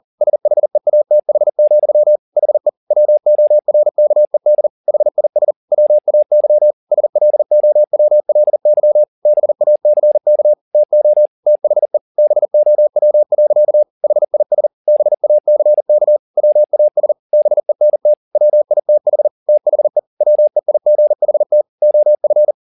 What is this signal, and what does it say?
Morse code, heard as a keyed tone, about 40 wpm, 600 Hz. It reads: SHEATH, HE WORKED HIS WAY SLOWLY BACK TO THE BOW. HIS BACK WAS BENT WITH THE WEIGHT OF
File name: the old man and the sea 2 40wpm